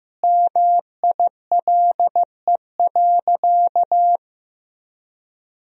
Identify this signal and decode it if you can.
Morse code, heard as a keyed tone, about 14 wpm, 700 Hz. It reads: MILE.
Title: Sentences from Top 1000 Words - Review Part 1 - 15wpm